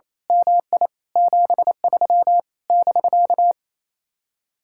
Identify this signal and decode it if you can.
Morse code, heard as a keyed tone, about 27 wpm, 700 Hz. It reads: MI 73 <BK>
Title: Speed-Racing Common QSO Phrases - 28wpm